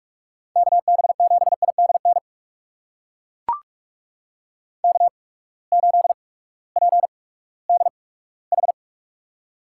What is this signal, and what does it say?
Morse code, heard as a keyed tone, about 42 wpm, spaced as for 20 wpm, 700 Hz. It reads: KB7IDN K8PDH
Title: USA Call Signs - 8x Character Spacing - 45wpm